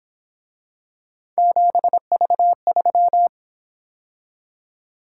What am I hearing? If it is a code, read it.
Morse code, heard as a keyed tone, about 27 wpm, 700 Hz. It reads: 7V3
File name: Speed-Racing - Three Letter-Number Combinations - 22wpm